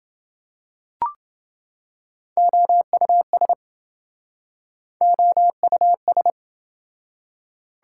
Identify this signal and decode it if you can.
Morse code, heard as a keyed tone, about 28 wpm, 700 Hz. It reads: OUS OUS